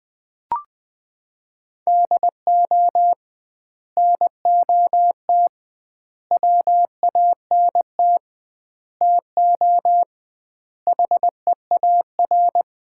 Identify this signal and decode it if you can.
Morse code, heard as a keyed tone, about 20 wpm, 700 Hz. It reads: DO NOT WANT TO HEAR
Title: 2x Word-Spacing - Sets of 5 Words - 20wpm